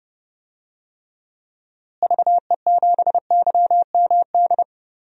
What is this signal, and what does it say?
Morse code, heard as a keyed tone, about 30 wpm, 700 Hz. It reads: VE7YMD
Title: DXCC-100 Call Signs - 6 Characters - Repeated 2x - 30wpm